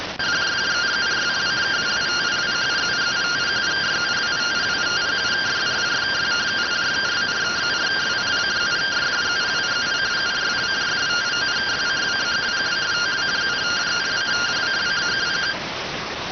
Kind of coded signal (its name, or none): two-tone teleprinter signal